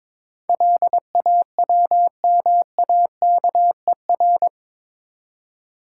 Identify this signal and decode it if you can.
Morse code, heard as a keyed tone, about 21 wpm, 700 Hz. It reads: LAWMAKER
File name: practice / 22wpm / Eight Letter Words - 22wpm